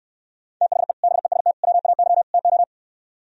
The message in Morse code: THE BEST WAY TO

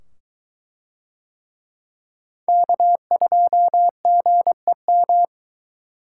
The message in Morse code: K2GEM

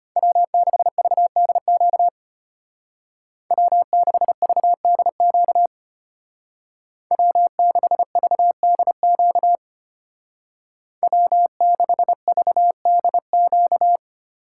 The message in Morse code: W6VDQ W6VDQ W6VDQ W6VDQ

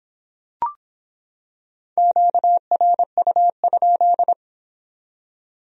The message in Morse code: QRU?

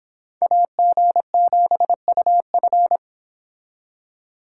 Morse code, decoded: AG7UF

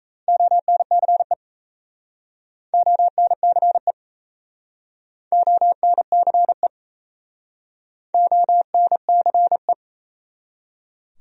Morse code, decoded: ONCE ONCE ONCE ONCE

